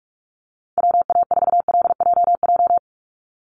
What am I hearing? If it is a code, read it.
Morse code, heard as a keyed tone, about 44 wpm, 700 Hz. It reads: WA4LJJ
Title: Call Signs Repeated 3x - 45wpm